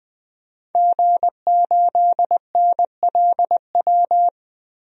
Morse code, heard as a keyed tone, about 20 wpm, 700 Hz. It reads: G8NLW
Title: DXCC-100 Call Signs - 5 Characters - Repeated 2x - 20wpm